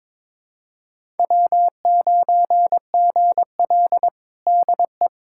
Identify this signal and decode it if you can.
Morse code, heard as a keyed tone, about 22 wpm, 700 Hz. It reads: W9GL DE
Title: Practice QSO - Exchange 4 - Ending - 22wpm